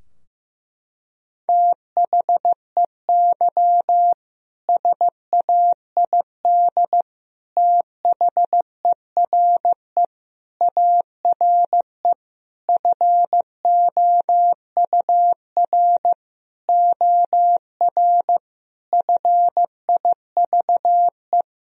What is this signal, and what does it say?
Morse code, heard as a keyed tone, about 15 wpm, 700 Hz. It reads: THEY SAID THERE ARE FOUR OR FIVE
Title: Sentences from Top 1000 Words - Review Part 2 - 15wpm